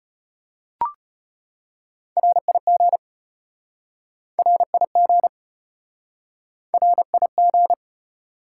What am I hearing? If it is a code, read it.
Morse code, heard as a keyed tone, about 34 wpm, 700 Hz. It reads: RIG RIG RIG